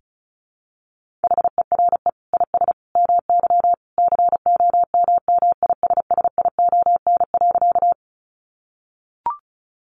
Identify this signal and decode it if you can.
Morse code, heard as a keyed tone, about 34 wpm, 700 Hz. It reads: HERE IS MY COMMISSION.
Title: Sentences from Top 1400 Words 35wpm